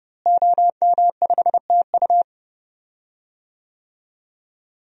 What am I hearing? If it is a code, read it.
Morse code, heard as a keyed tone, about 32 wpm, 700 Hz. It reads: OM5TU
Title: DXCC-100 Call Signs - 5 Characters - Repeated 3x - 30wpm